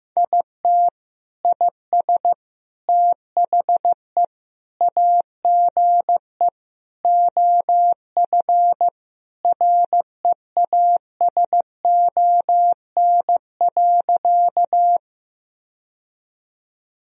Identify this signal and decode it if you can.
Morse code, heard as a keyed tone, about 16 wpm, 700 Hz. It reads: IT IS THE AGE OF REASON.
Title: Sentences from Top 1000 Words - Review Part 3 - 15wpm